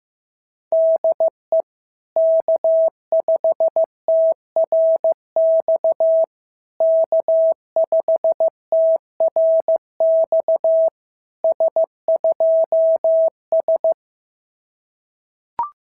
Morse code, heard as a keyed tone, about 14 wpm, 650 Hz. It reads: DE K5TRX K5TRX S2S
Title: SOTA QSO Practice 15wpm